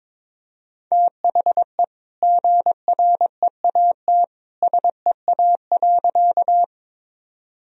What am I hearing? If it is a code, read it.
Morse code, heard as a keyed tone, about 22 wpm, 700 Hz. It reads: THE GREAT SEA.
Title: Sentences from Top 1000 Words - Review Part 1 - 22wpm